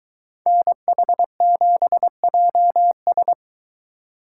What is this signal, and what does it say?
Morse code, heard as a keyed tone, about 23 wpm, 700 Hz. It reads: NH7JS